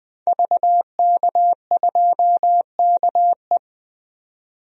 Morse code, heard as a keyed tone, about 20 wpm, 700 Hz. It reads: VK2KE
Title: DXCC-100 Call Signs - 5 Characters - 20wpm